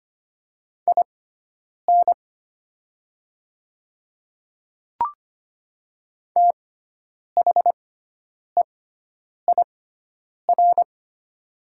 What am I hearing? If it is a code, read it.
Morse code, heard as a keyed tone, about 25 wpm, spaced as for 9 wpm, 700 Hz. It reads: IN THEIR
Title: 6x Character Spacing - Top 200 Words - 25wpm